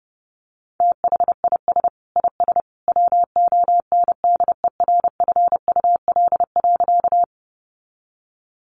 Morse code, heard as a keyed tone, about 30 wpm, 700 Hz. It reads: THIS IS WONDERFUL.
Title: Sentences from Top 1100 Words 30wpm